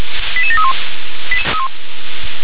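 Five-tone selective call (ZVEI), heard as quick runs of five tones, tones of about 70 ms. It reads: 88731
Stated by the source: Five-tone ZVEI selective call received by SDR.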